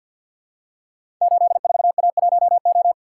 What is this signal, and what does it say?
Morse code, heard as a keyed tone, about 51 wpm, 700 Hz. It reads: 84A1O
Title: Code Groups - 5 Letter:Numbers - 50wpm